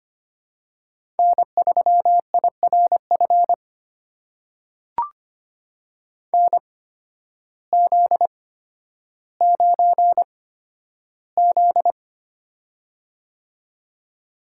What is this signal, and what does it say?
Morse code, heard as a keyed tone, about 25 wpm, spaced as for 12 wpm, 700 Hz. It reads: N3IRF NZ9Z
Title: USA Call Signs - 8x Character Spacing - 25wpm